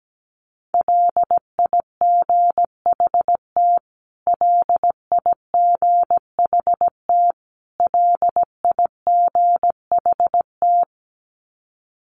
Morse code, heard as a keyed tone, about 17 wpm, 700 Hz. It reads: LIGHT LIGHT LIGHT
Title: Top 200 Words Repeated 17wpm